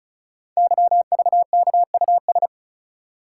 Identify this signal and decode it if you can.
Morse code, heard as a keyed tone, about 36 wpm, 700 Hz. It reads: YVKUS
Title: Code Groups - 5 Letters - 35wpm